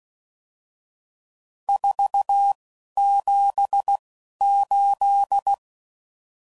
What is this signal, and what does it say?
Morse code, heard as a keyed tone, about 16 wpm, spaced as for 11 wpm, 800 Hz. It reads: 478
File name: CW Code Groups @10WPM / RSTs / Various RST No's